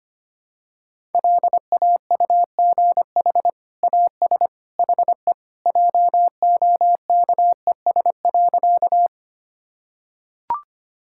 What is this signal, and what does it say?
Morse code, heard as a keyed tone, about 25 wpm, 700 Hz. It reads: LAUGH AS HE JOKES.
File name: practice / 25wpm / Sentences with 2-Word Phrases - 25wpm